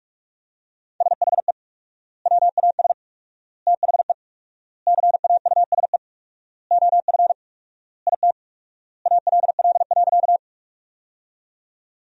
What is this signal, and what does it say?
Morse code, heard as a keyed tone, about 46 wpm, 700 Hz. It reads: SHE WAS THE CAUSE OF IT ALL.
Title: 4x Word-Spacing - Sentences from Top 200 Words - 45wpm